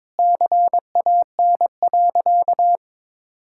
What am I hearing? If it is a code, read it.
Morse code, heard as a keyed tone, about 22 wpm, 700 Hz. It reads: CAN.
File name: practice / 22wpm / Sentences with 2-Word Phrases - 22wpm